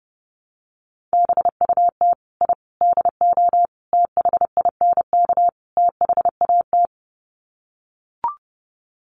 Morse code, heard as a keyed tone, about 30 wpm, 700 Hz. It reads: BUT I DO THINK THAT
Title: Sets of 5 Words Encore2 30wpm